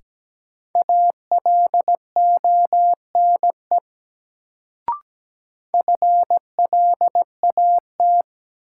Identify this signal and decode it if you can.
Morse code, heard as a keyed tone, about 17 wpm, 700 Hz. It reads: ALONE FLAT
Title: Sentences from Top 1000 Words - New Word Drill - 17wpm